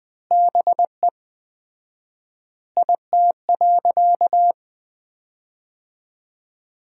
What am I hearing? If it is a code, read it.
Morse code, heard as a keyed tone, about 20 wpm, 700 Hz. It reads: BE IT.
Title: 4x Word-Spacing - Sentences from Top 100 Words - 20wpm